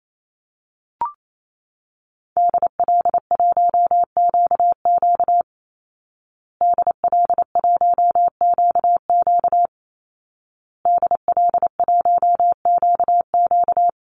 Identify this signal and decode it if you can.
Morse code, heard as a keyed tone, about 28 wpm, 700 Hz. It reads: DL1QQ DL1QQ DL1QQ
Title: Call Signs Repeated 2x - 28wpm